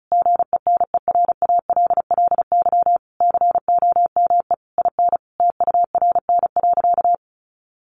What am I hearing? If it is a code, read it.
Morse code, heard as a keyed tone, about 34 wpm, 700 Hz. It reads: GENERALLY COME IN TURN.
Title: Sentences from Top 1500 Words 35wpm